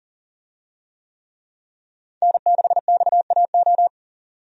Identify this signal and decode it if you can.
Morse code, heard as a keyed tone, about 38 wpm, 700 Hz. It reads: N6XAO
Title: USA Call Signs - 5 Characters - 40wpm